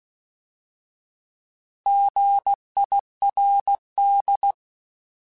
Morse code, heard as a keyed tone, about 16 wpm, 800 Hz. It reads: GIRD
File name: Alphabetical / Alphabetical mp3s / Gs @15